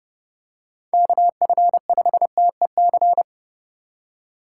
Morse code, heard as a keyed tone, about 30 wpm, 700 Hz. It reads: KF5TEC
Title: USA Call Signs - 4x Character Spacing - 30wpm